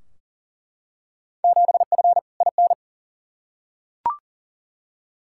Morse code, heard as a keyed tone, about 38 wpm, 700 Hz. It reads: 7F IN
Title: Field Day Exchanges Encore - 40wpm